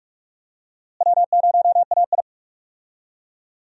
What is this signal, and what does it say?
Morse code, heard as a keyed tone, about 46 wpm, 700 Hz. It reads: W0AI
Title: USA Call Signs - 4 Characters - Repeated 2x - 45wpm